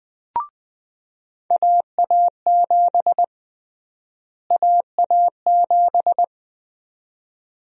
Morse code, heard as a keyed tone, about 20 wpm, 700 Hz. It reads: AA7 AA7
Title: USA Call Sign Prefixes - Repeated 2x - 20wpm